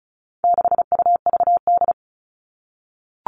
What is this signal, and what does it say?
Morse code, heard as a keyed tone, about 34 wpm, 700 Hz. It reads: 6UVD
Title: All Cut Numbers - 4-Digit 35wpm